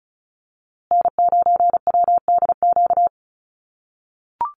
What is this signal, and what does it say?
Morse code, heard as a keyed tone, about 34 wpm, 700 Hz. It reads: N9WDQ
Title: USA Call Signs Repeated 2x 35wpm